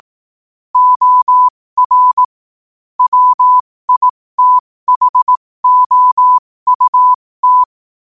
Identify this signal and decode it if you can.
Morse code, heard as a keyed tone, about 18 wpm, spaced as for 15 wpm, 1000 Hz. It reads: OR WITHOUT
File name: qotdfiles / 15wpm